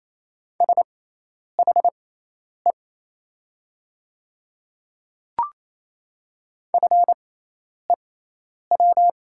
Morse code, heard as a keyed tone, about 28 wpm, spaced as for 10 wpm, 700 Hz. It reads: SHE FEW
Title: 6x Character Spacing - Top 500 Words - 28wpm